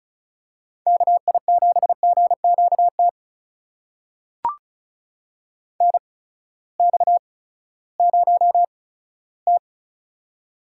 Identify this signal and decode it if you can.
Morse code, heard as a keyed tone, about 36 wpm, spaced as for 18 wpm, 700 Hz. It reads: KI7GQT NX0T
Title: USA Call Signs - 8x Character Spacing - 35wpm